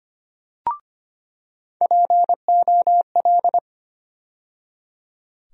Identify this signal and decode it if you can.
Morse code, heard as a keyed tone, about 25 wpm, 700 Hz. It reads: POL
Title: USA Call Sign Postfixes - 3 Letters 25wpm